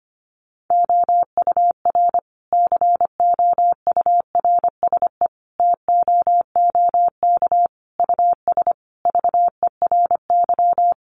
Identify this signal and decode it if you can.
Morse code, heard as a keyed tone, about 25 wpm, 700 Hz. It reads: OUR COURSE TOOK US VERY
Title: Sentences from Top 1300 Words 25wpm